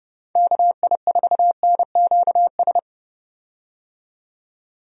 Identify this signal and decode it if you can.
Morse code, heard as a keyed tone, about 30 wpm, 700 Hz. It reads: KI4NQS
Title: USA Call Signs - 6 Characters - 30wpm